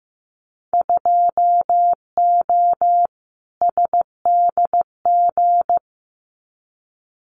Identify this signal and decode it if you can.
Morse code, heard as a keyed tone, about 15 wpm, 700 Hz. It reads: 2O SDG
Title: Winter Field Day Exchanges 15wpm